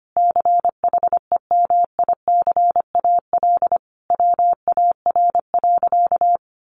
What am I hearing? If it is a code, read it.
Morse code, heard as a keyed tone, about 25 wpm, 700 Hz. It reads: CHEMICAL WAR.
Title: Sentences from Top 1200 Words 25wpm